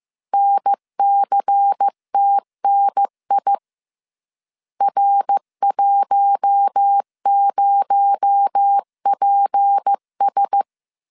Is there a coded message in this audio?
Yes: Morse code from 0.3 to 10.6 s, heard as a keyed tone, about 15 wpm, 800 Hz. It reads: NCTNI R10PS